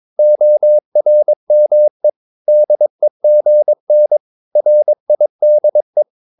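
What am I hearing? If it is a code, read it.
Morse code, heard as a keyed tone, about 22 wpm, 600 Hz. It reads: ORME DEGN RIDE
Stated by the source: Morse practice of short Danish words at 22 wpm, from CW med Gnister.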